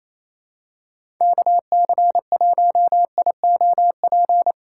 Morse code, heard as a keyed tone, about 28 wpm, 700 Hz. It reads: KC1IOP
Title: USA Call Signs - 3x Character Spacing - 28wpm